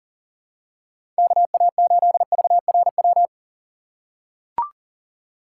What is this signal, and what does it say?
Morse code, heard as a keyed tone, about 40 wpm, 700 Hz. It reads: KA8VRW
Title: USA Call Signs - 6 Characters - 2x Character Spacing - 40wpm